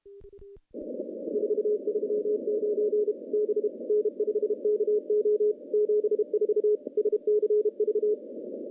Morse code, heard as a keyed tone, about 31 wpm, 420 Hz. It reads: 439 B N5KO 74 SCV
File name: contests / CW-SS-k5tr- -0600